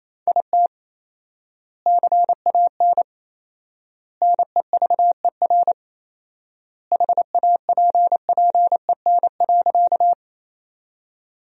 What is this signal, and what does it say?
Morse code, heard as a keyed tone, about 28 wpm, 700 Hz. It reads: IT CAN NEVER HAPPEN.
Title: 4x Word-Spacing - Sentences from Top 400 Words - 28wpm